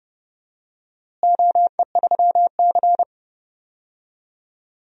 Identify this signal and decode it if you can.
Morse code, heard as a keyed tone, about 30 wpm, 700 Hz. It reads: OE3C
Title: DXCC-100 Call Signs - 4 Characters - Repeated 2x - 30wpm